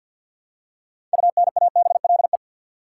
Morse code, heard as a keyed tone, about 53 wpm, 700 Hz. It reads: UNABLE
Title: Six Letter Words - 50wpm